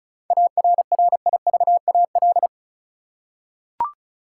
Morse code, heard as a keyed tone, about 36 wpm, 700 Hz. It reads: ARRIVAL
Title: Seven Letter Words - 35wpm